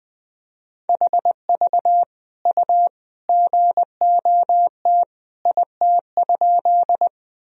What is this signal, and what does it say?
Morse code, heard as a keyed tone, about 20 wpm, 700 Hz. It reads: HV U GOT IT?
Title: Sentences from Top 300 Words and Abbreviations 20wpm